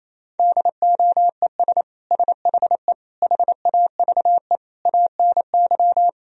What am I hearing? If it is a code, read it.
Morse code, heard as a keyed tone, about 28 wpm, 700 Hz. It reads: DOES SHE HAVE ANY
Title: Sentences from Top 1000 Words - Review Part 3 - 28wpm